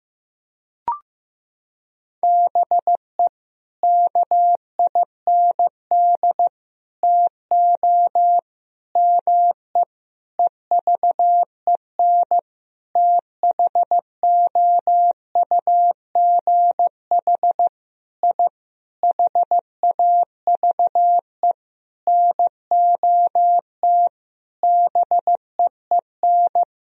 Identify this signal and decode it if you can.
Morse code, heard as a keyed tone, about 15 wpm, 700 Hz. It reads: BE KIND TO ME EVEN THOUGH I HAVE NOT BEEN
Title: Sentences from Top 1000 Words - Review Part 2 - 15wpm